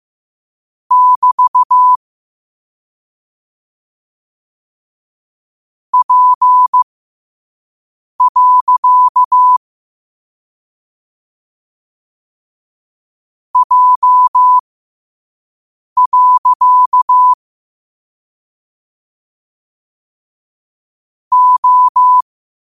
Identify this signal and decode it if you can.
Morse code, heard as a keyed tone, about 15 wpm, spaced as for 5 wpm, 1000 Hz. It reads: <BT> P. J. O
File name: qotdfiles / 5wpm